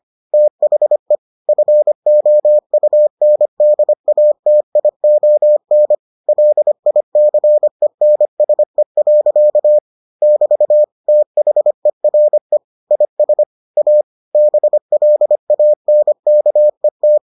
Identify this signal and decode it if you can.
Morse code, heard as a keyed tone, about 25 wpm, 600 Hz. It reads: THE FOUNDATION LICENSE. <BT> THERE IS A BLANKET